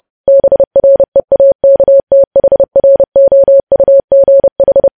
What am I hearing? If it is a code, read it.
Morse code, heard as a keyed tone, about 30 wpm, 550 Hz. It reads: BREAKTHROUGH
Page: news headlines in morse code at 30 wpm